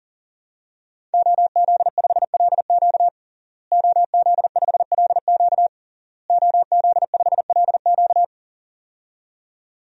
Morse code, heard as a keyed tone, about 40 wpm, 700 Hz. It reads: OZ5LQ OZ5LQ OZ5LQ
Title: DXCC-100 Call Signs - 5 Characters - Repeated 3x - 40wpm